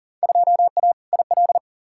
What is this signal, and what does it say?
Morse code, heard as a keyed tone, about 40 wpm, 700 Hz. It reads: 2A IL
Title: Field Day Exchanges Encore - 40wpm